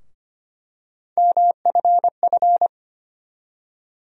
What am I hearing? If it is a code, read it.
Morse code, heard as a keyed tone, about 25 wpm, 700 Hz. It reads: MFF
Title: USA Call Sign Postfixes - 3 Letters 25wpm